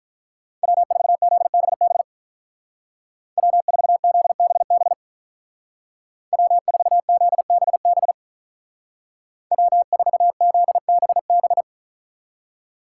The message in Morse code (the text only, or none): W4ZBB W4ZBB W4ZBB W4ZBB